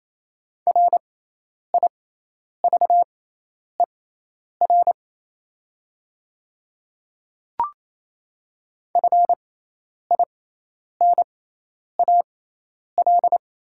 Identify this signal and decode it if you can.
Morse code, heard as a keyed tone, about 28 wpm, spaced as for 10 wpm, 700 Hz. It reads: RIVER FINAL
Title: 6x Character Spacing - Top 500 Words - 28wpm